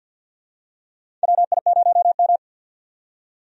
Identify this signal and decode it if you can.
Morse code, heard as a keyed tone, about 51 wpm, 700 Hz. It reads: WI0M
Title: USA Call Signs - 4 Characters - 50wpm